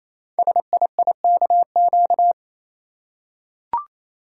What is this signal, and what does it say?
Morse code, heard as a keyed tone, about 29 wpm, 700 Hz. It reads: SIIKQ
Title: Code Groups - 5 Letters - 28wpm